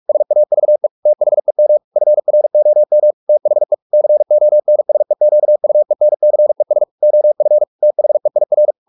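Morse code, heard as a keyed tone, about 44 wpm, 600 Hz. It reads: SAVE THEM FROM THE CONSEQUENCES OF THEIR
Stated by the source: Morse practice at 45 wpm, the novel Never by Ken Follett in English, from CW med Gnister.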